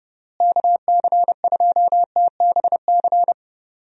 Morse code, heard as a keyed tone, about 30 wpm, 700 Hz. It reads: KC2TBC
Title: USA Call Signs - 6 Characters - Repeated 3x - 30wpm